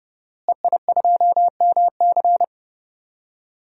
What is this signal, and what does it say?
Morse code, heard as a keyed tone, about 30 wpm, 700 Hz. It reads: EI2MC
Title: DXCC-100 Call Signs - 5 Characters - Repeated 2x - 30wpm